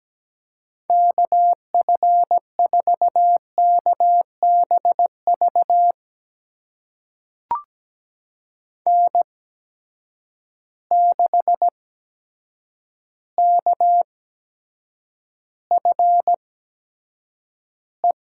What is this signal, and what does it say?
Morse code, heard as a keyed tone, about 17 wpm, spaced as for 8 wpm, 700 Hz. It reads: KF4KBV N6KFE
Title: USA Call Signs - 8x Character Spacing - 17wpm